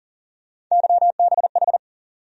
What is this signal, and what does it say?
Morse code, heard as a keyed tone, about 40 wpm, 700 Hz. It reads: YBH
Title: USA Call Sign Postfixes - 3 Letters 40wpm